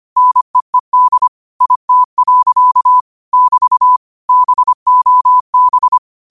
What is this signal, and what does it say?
Morse code, heard as a keyed tone, about 25 wpm, 1000 Hz. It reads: NEED IT. <BT> BOB